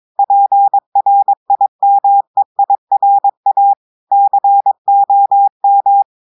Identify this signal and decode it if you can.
Morse code, heard as a keyed tone, about 22 wpm, 800 Hz. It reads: PRIMEIRA COM
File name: audios / portugais / portugais texte en clair 1 vitesse 22 mots minutes